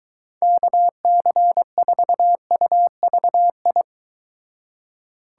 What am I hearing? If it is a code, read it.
Morse code, heard as a keyed tone, about 23 wpm, 700 Hz. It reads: KC4UVI